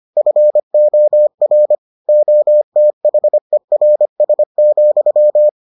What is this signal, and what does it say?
Morse code, heard as a keyed tone, about 25 wpm, 600 Hz. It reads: FOR OTHERS,